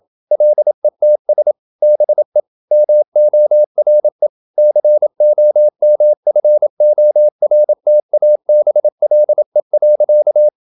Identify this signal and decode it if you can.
Morse code, heard as a keyed tone, about 27 wpm, 600 Hz. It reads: LETS BE MORE COMFORTABLE.
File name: Never 0027wpm